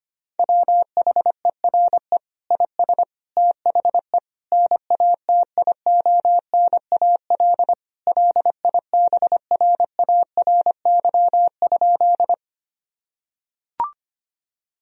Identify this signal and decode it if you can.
Morse code, heard as a keyed tone, about 25 wpm, 700 Hz. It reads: WHERE IS THE NATIONAL LIBRARY?
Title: Sentences from Top 1000 Words - Review Part 4 - 25wpm